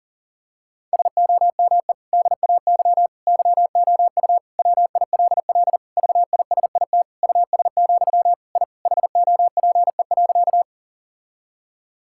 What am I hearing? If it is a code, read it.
Morse code, heard as a keyed tone, about 40 wpm, 700 Hz. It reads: SOME DAY YOU WILL VISIT US, I HOPE.
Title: Sentences from Top 1000 Words - Review Part 6 - 40wpm